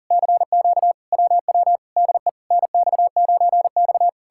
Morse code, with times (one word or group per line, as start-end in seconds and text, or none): 0.1-0.9 CQ
1.1-1.8 WW
2.0-2.3 DE
2.5-4.1 NX9X